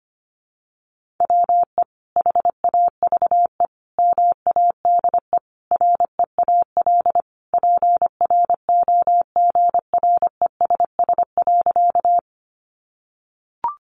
Morse code, heard as a keyed tone, about 25 wpm, 700 Hz. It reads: WE HAVE MADE REAL PROGRESS.
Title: Sentences from Top 1400 Words 25wpm